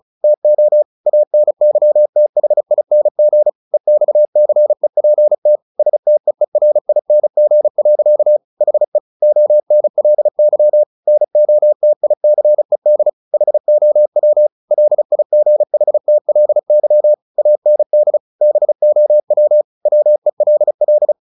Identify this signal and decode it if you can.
Morse code, heard as a keyed tone, about 34 wpm, 600 Hz. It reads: TO ANYTHING EXCEPT STEERING. HE ONLY NOTICED HOW LIGHTLY AND BOW WELL
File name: the old man and the sea 2 35wpm